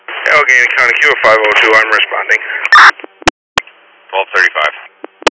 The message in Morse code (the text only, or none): none